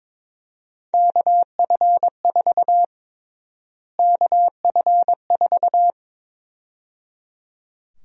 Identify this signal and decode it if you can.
Morse code, heard as a keyed tone, about 22 wpm, 700 Hz. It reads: KF4 KF4
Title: USA Call Sign Prefixes - Repeated 2x - 22wpm